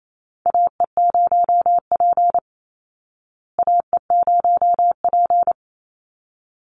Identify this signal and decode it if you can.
Morse code, heard as a keyed tone, about 28 wpm, 700 Hz. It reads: AE0P AE0P
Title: Call Signs Repeated 3x - 28wpm